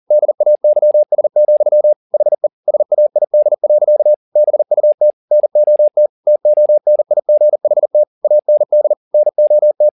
Morse code, heard as a keyed tone, about 40 wpm, 600 Hz. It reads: DAYS, HE SAID. BUT NOT TONIGHT AND NOT
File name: the old man and the sea 2 40wpm